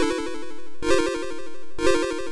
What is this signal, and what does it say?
There is no code.